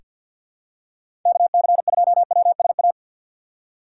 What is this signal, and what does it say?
Morse code, heard as a keyed tone, about 47 wpm, 700 Hz. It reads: KC2WSA
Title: USA Call Signs - 4x Character Spacing - 50wpm